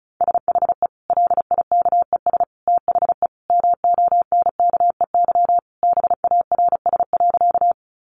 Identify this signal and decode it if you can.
Morse code, heard as a keyed tone, about 34 wpm, 700 Hz. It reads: SHE LIKES THE MONKEY BARS.
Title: Easy Phrases 35wpm